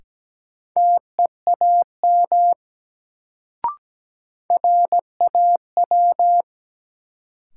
Morse code, heard as a keyed tone, about 17 wpm, 700 Hz. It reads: TEAM RAW
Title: Sentences from Top 1500 Words - New Word Drill Alt - 17wpm